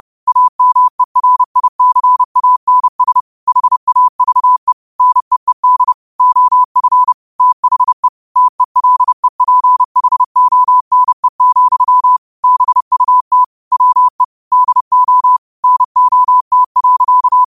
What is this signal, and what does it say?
Morse code, heard as a keyed tone, about 30 wpm, 1000 Hz. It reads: AMERICANS HAVE NEED OF THE TELEPHONE, BUT WE DO NOT.
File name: qotdfiles / 30wpm